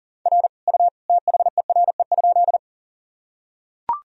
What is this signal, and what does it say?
Morse code, heard as a keyed tone, about 40 wpm, 700 Hz. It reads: R U THERE?
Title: Sentences from Top 100 Words and Abbreviations 40wpm